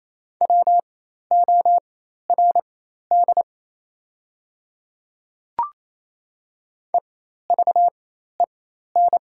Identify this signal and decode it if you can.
Morse code, heard as a keyed tone, about 28 wpm, spaced as for 13 wpm, 700 Hz. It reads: WORD EVEN
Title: 4x Character Spacing - Top 200 Words - 28wpm